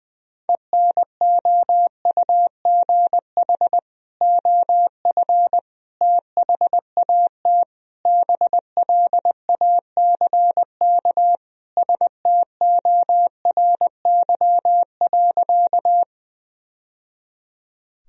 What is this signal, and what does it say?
Morse code, heard as a keyed tone, about 21 wpm, 700 Hz. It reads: ENOUGH OF THAT BLACK STORY.
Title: Sentences from Top 1000 Words - Review Part 2 - 20wpm